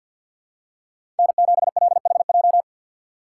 Morse code, heard as a keyed tone, about 52 wpm, 700 Hz. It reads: N7LHJ
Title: USA Call Signs - 4x Character Spacing - 50wpm